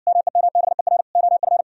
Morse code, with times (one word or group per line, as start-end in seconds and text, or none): 0.1-1.0 MEMBER
1.2-1.6 OF